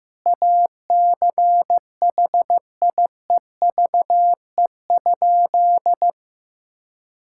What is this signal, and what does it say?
Morse code, heard as a keyed tone, about 15 wpm, 700 Hz. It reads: ACHIEVE?
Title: Sentences from Top 1000 Words - Review Part 6 - 15wpm